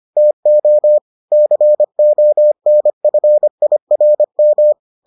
Morse code, heard as a keyed tone, about 25 wpm, 600 Hz. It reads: TO CONFIRM